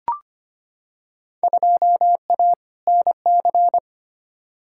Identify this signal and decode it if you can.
Morse code, heard as a keyed tone, about 25 wpm, 700 Hz. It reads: 2A NC
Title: Field Day Exchanges - 25wpm